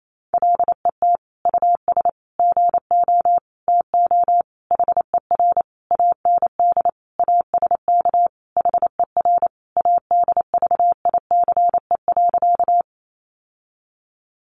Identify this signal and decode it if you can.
Morse code, heard as a keyed tone, about 28 wpm, 700 Hz. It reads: LET US GO TO HER AND ASK HER ADVICE.
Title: Sentences from Top 1100 Words 28wpm